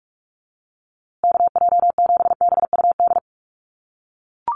Morse code, heard as a keyed tone, about 44 wpm, 700 Hz. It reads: KJ7BUD
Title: USA Call Signs Repeated 2x 45wpm